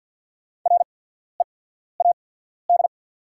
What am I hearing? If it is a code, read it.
Morse code, heard as a keyed tone, about 48 wpm, spaced as for 14 wpm, 690 Hz. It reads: READ
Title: 8x Character Spacing - Top 500 Words - 50wpm